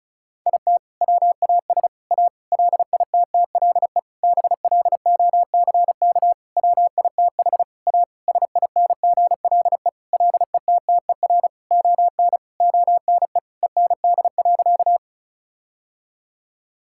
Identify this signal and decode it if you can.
Morse code, heard as a keyed tone, about 36 wpm, 700 Hz. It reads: IT WAS A LITTLE BLOCK WITH A SINGLE LETTER ON ONE END.
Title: Sentences from Top 1600 Words 35wpm